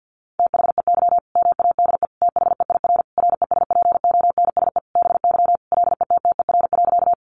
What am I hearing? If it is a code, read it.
Morse code, heard as a keyed tone, about 48 wpm, 700 Hz. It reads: THEY MADE THEIR RESPONSE BY LETTER.
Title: Sentences from Top 1400 Words 50wpm